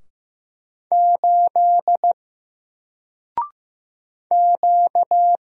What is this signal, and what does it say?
Morse code, heard as a keyed tone, about 15 wpm, 700 Hz. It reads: 8 Q
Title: Single Letter-Number Alt 15wpm